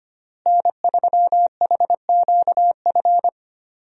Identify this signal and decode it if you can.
Morse code, heard as a keyed tone, about 25 wpm, 700 Hz. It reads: N3HQF
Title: USA Call Signs - 5 Characters - Repeated 2x - 25wpm